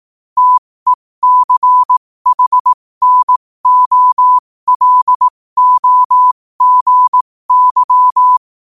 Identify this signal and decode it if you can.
Morse code, heard as a keyed tone, about 18 wpm, spaced as for 15 wpm, 1000 Hz. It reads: TECHNOLOGY
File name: qotdfiles / 15wpm